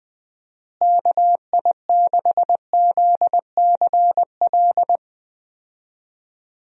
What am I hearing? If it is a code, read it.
Morse code, heard as a keyed tone, about 20 wpm, 700 Hz. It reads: KI6ZCL
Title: USA Call Signs - 6 Characters - Repeated 2x - 20wpm